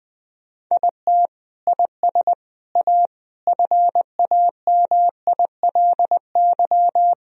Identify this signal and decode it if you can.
Morse code, heard as a keyed tone, about 19 wpm, 700 Hz. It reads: IT IS A FAMILY
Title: Sentences from Top 1000 Words - Review Part 2 - 20wpm